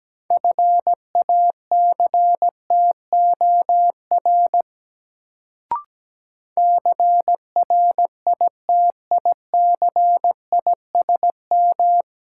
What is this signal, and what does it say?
Morse code, heard as a keyed tone, about 17 wpm, 700 Hz. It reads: FACTOR CRITICISM
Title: Sentences from Top 1300 Words - New Word Drill - 17wpm